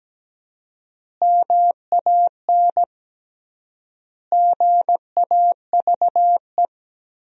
Morse code, heard as a keyed tone, about 17 wpm, 700 Hz. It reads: MAN GAVE
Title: 3x Word-Spacing - Sentences from Top 500 Words - 17wpm